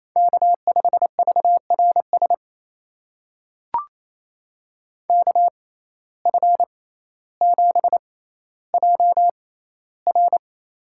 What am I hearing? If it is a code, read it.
Morse code, heard as a keyed tone, about 28 wpm, spaced as for 14 wpm, 700 Hz. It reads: K5VRS KF7JR
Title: USA Call Signs - 5 Characters - 6x Character Spacing - 28wpm